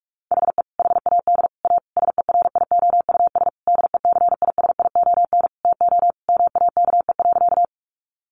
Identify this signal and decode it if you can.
Morse code, heard as a keyed tone, about 44 wpm, 700 Hz. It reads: HE HAD A SERIOUS DECISION TO MAKE.
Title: Sentences from Top 1100 Words 45wpm